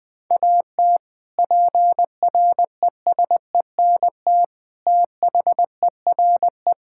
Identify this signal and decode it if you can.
Morse code, heard as a keyed tone, about 21 wpm, 700 Hz. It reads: AT PRESENT THERE
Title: Sentences from Top 1000 Words - Review Part 5 - 20wpm